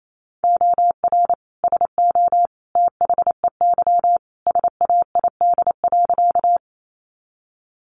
Morse code, heard as a keyed tone, about 28 wpm, 700 Hz. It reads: OR SO THEY SAID.
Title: Sentences from Top 100 Words 28wpm